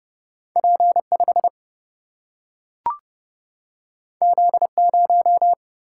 Morse code, heard as a keyed tone, about 29 wpm, 700 Hz. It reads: P5 Z0
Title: Single-Letter Single-Number 30wpm